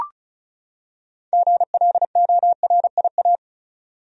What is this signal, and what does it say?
Morse code, heard as a keyed tone, about 36 wpm, 700 Hz. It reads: GLORIA
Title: USA Common Names - 35wpm